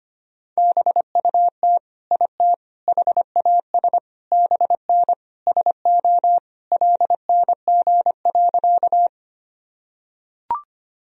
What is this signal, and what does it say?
Morse code, heard as a keyed tone, about 25 wpm, 700 Hz. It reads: BUT IT HAS BN SO LNG.
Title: Sentences from Top 100 Words and Abbreviations 25wpm